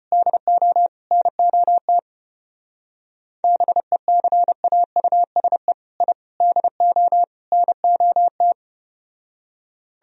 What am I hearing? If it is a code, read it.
Morse code, heard as a keyed tone, about 33 wpm, 700 Hz. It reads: DO NOT BECAUSE I DO NOT